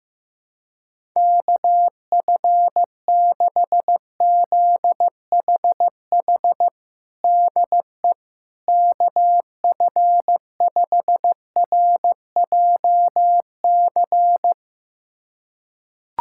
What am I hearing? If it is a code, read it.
Morse code, heard as a keyed tone, about 16 wpm, 700 Hz. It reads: KF6ZHH DE KF5RJC
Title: USA Callsign DE Callsign 15wpm